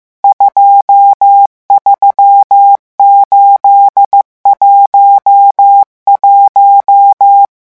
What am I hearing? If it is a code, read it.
Morse code, heard as a keyed tone, about 15 wpm, 800 Hz. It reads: 23811
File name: morse mp3 / revision11 chiffres vitesse 14